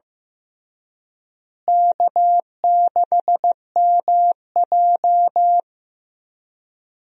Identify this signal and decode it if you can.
Morse code, heard as a keyed tone, about 15 wpm, 700 Hz. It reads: K6MJ
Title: USA Call Signs - 4 Characters - Repeated 3x - 15wpm